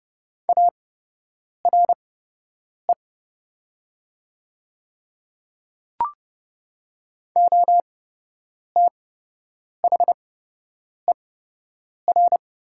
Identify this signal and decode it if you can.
Morse code, heard as a keyed tone, about 31 wpm, spaced as for 8 wpm, 700 Hz. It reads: ARE OTHER
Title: 8x Character Spacing - Top 100 Words - 30wpm